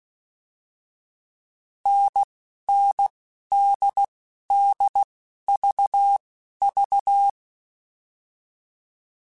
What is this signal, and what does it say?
Morse code, heard as a keyed tone, about 16 wpm, spaced as for 11 wpm, 800 Hz. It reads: NNDDVV